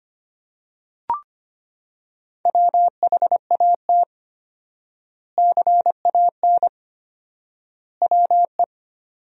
Morse code, heard as a keyed tone, about 25 wpm, 700 Hz. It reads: WHAT CAN WE
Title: 4x Word-Spacing - Sentences from Top 100 Words - 25wpm